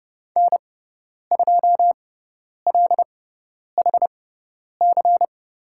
Morse code, heard as a keyed tone, about 30 wpm, spaced as for 10 wpm, 700 Hz. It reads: N2LHC
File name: Lesson 16g - Letter c - Call Signs 30wpm